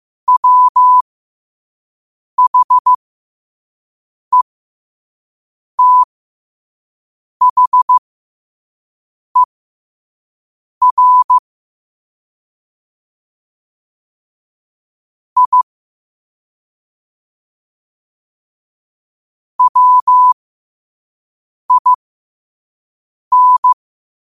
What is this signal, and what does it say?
Morse code, heard as a keyed tone, about 15 wpm, spaced as for 5 wpm, 1000 Hz. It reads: WHETHER I WIN
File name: qotdfiles / 5wpm